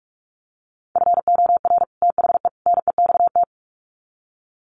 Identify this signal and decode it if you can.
Morse code, heard as a keyed tone, about 44 wpm, 700 Hz. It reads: FOR THE NEXT
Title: Sets of 3 Words 45wpm